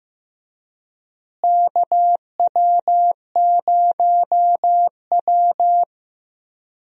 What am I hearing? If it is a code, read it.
Morse code, heard as a keyed tone, about 15 wpm, 700 Hz. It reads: KW0W
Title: USA Call Signs - 4 Characters - Repeated 3x - 15wpm